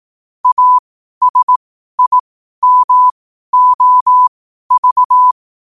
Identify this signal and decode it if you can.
Morse code, heard as a keyed tone, about 18 wpm, spaced as for 12 wpm, 1000 Hz. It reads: ASIMOV
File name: qotdfiles / 12wpm